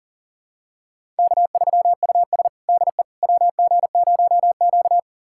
Morse code, heard as a keyed tone, about 40 wpm, 700 Hz. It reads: K3US DE WG0Q
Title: USA Callsign DE Callsign - 4 Characters - 40wpm